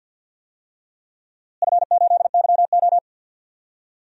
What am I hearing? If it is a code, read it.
Morse code, heard as a keyed tone, about 51 wpm, 700 Hz. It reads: F8YO
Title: DXCC-100 Call Signs - 4 Characters - Repeated 2x - 50wpm